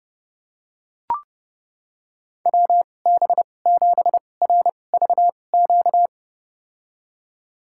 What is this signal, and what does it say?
Morse code, heard as a keyed tone, about 30 wpm, spaced as for 22 wpm, 700 Hz. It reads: WB7RVQ
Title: USA Call Signs - 6 Characters - 2x Character Spacing - 30wpm